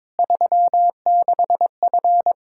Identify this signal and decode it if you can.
Morse code, heard as a keyed tone, about 22 wpm, 700 Hz. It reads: 36F
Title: Practice QSO - Exchange 2 - Info - 22wpm